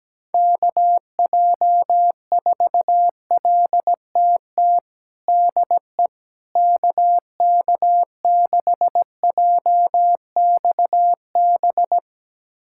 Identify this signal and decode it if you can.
Morse code, heard as a keyed tone, about 17 wpm, 700 Hz. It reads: KJ4LTT DE KK6JXB
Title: USA Callsign DE Callsign 17wpm